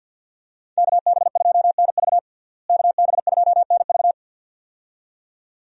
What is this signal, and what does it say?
Morse code, heard as a keyed tone, about 53 wpm, 700 Hz. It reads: KB2NV KB2NV
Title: USA Call Signs - 5 Characters - Repeated 2x - 50wpm